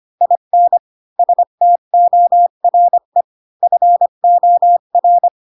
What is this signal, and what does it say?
Morse code, heard as a keyed tone, about 25 wpm, 700 Hz. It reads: IN STORE FOR
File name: Pinocchio / Pinocchio ch7 20 25 ik1hge